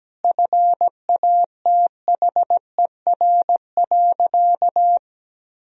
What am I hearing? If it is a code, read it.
Morse code, heard as a keyed tone, about 17 wpm, 700 Hz. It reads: FATHER.
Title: Sentences from Top 500 Words and Abbreviations 17wpm